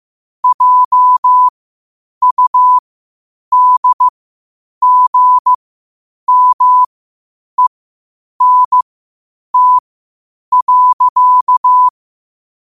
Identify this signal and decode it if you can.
Morse code, heard as a keyed tone, about 15 wpm, spaced as for 8 wpm, 1000 Hz. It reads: JUDGMENT.
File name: qotdfiles / 8wpm